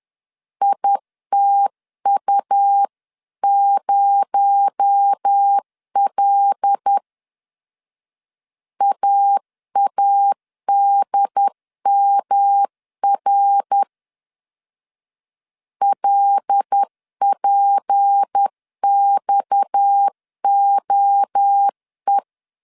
Morse code, heard as a keyed tone, about 10 wpm, 800 Hz. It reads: ITU0L AADMR LPXOE
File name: CW / DET / 1 6 1500C S C 10wpm 4252